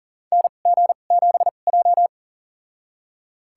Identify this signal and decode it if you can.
Morse code, heard as a keyed tone, about 40 wpm, spaced as for 29 wpm, 700 Hz. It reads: NG7J
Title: USA Call Signs - 4 Characters - 2x Character Spacing - 40wpm